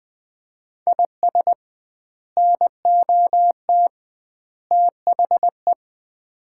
Morse code, heard as a keyed tone, about 20 wpm, 700 Hz. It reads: IS NOT THE
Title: 2x Word-Spacing - Sets of 3 Words - 20wpm